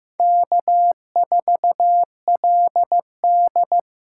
Morse code, heard as a keyed tone, about 15 wpm, 700 Hz. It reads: K4LD
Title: USA Call Signs - 4 Characters - 4x Character Spacing - 15wpm